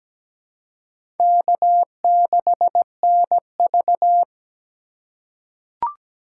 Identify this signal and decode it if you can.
Morse code, heard as a keyed tone, about 17 wpm, 700 Hz. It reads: K6NV
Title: USA Call Signs - 4 Characters - 2x Character Spacing - 17wpm